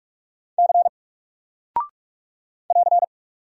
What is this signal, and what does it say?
Morse code, heard as a keyed tone, about 45 wpm, 700 Hz. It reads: C <AR>
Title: Single Character (Letter:Number:Punctuation:Prosign) Alt 45wpm